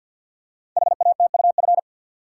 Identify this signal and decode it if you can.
Morse code, heard as a keyed tone, about 52 wpm, 700 Hz. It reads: HATUF
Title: Code Groups - 5 Letters - 50wpm